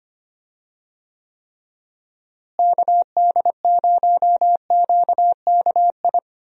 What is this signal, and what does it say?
Morse code, heard as a keyed tone, about 25 wpm, 700 Hz. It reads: KD0QKI
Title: USA Call Signs - 6 Characters - Repeated 2x - 25wpm